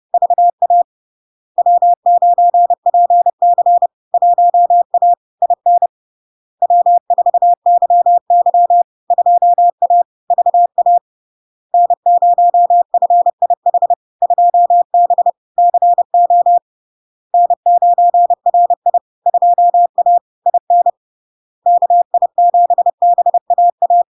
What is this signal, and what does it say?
Morse code, heard as a keyed tone, about 30 wpm, 700 Hz. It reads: VA W9PC 1A IN W4YY 2A VA N0FIH 2B CO N9RI 2A IN KI7BAA
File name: LARK-FD-2022-Log-in-CW-30wpm-700-Hz